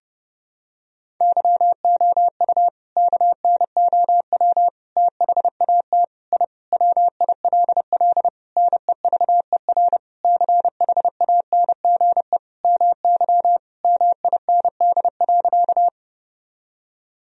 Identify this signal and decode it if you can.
Morse code, heard as a keyed tone, about 30 wpm, 700 Hz. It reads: YOU KNOW THAT I WILL NEVER CHANGE MY MIND.
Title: Sentences from Top 1000 Words - Review Part 1 - 30wpm